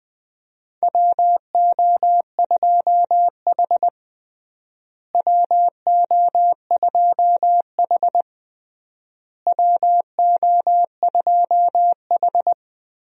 Morse code, heard as a keyed tone, about 20 wpm, 700 Hz. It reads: WO2H WO2H WO2H
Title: USA Call Signs - 4 Characters - Repeated 3x - 20wpm